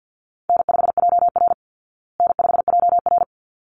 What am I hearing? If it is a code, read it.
Morse code, heard as a keyed tone, about 49 wpm, 700 Hz. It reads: N5JR N5JR
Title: Call Signs Repeated 3x - 50wpm